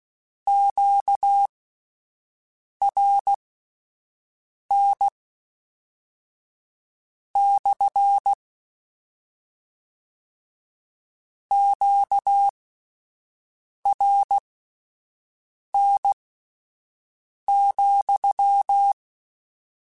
Morse code, heard as a keyed tone, about 16 wpm, spaced as for 5 wpm, 800 Hz. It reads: QRN / QRN,